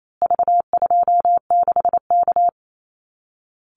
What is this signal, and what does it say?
Morse code, heard as a keyed tone, about 28 wpm, 700 Hz. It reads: V26K
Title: Call Signs Repeated 2x - 28wpm